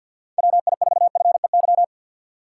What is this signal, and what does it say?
Morse code, heard as a keyed tone, about 52 wpm, 700 Hz. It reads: WI4FEY
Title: USA Call Signs - 6 Characters - 3x Character Spacing - 50wpm